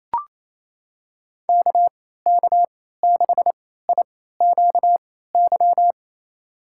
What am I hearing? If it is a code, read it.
Morse code, heard as a keyed tone, about 28 wpm, spaced as for 16 wpm, 700 Hz. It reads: KK6IQY
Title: USA Call Signs - 6 Characters - 3x Character Spacing - 28wpm